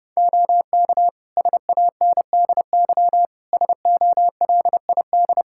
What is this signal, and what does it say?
Morse code, heard as a keyed tone, about 30 wpm, 700 Hz. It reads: OK SANDY SOLID
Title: Practice QSO - Exchange 3 - More Info - 30wpm